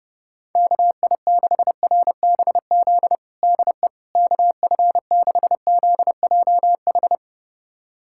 Morse code, heard as a keyed tone, about 30 wpm, 700 Hz. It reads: KI6RBZ DE KF6ZJH
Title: USA Callsign DE Callsign - 6 Characters - 30wpm